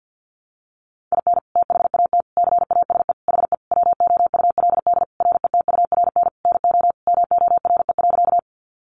Morse code, heard as a keyed tone, about 50 wpm, 700 Hz. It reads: IN THAT CASE HE WOULD RETURN NO MORE.
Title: Sentences from Top 800 Words 50wpm